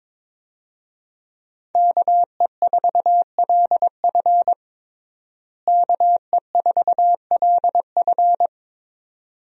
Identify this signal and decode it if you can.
Morse code, heard as a keyed tone, about 22 wpm, 700 Hz. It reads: KE4LF KE4LF
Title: USA Call Signs - 5 Characters - Repeated 2x - 22wpm